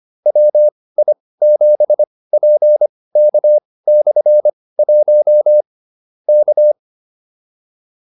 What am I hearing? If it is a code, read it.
Morse code, heard as a keyed tone, about 25 wpm, spaced as for 18 wpm, 600 Hz. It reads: WI7PK/1 K